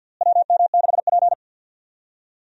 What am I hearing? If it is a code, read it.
Morse code, heard as a keyed tone, about 52 wpm, 700 Hz. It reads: WM6P
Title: USA Call Signs - 4 Characters - Repeated 3x - 50wpm